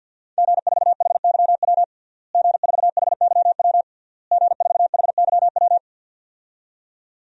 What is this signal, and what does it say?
Morse code, heard as a keyed tone, about 47 wpm, 700 Hz. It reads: G4HYW G4HYW G4HYW
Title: DXCC-100 Call Signs - 5 Characters - Repeated 3x - 50wpm